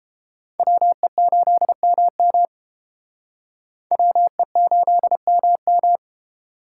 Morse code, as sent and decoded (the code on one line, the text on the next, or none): .-- . ---.. -- -- / .-- . ---.. -- --
WE8MM WE8MM